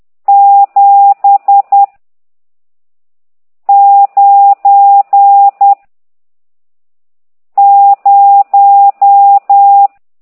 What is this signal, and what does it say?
Morse code, heard as a keyed tone, about 10 wpm, 800 Hz. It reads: 7 9 0
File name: ct2idw / exerc14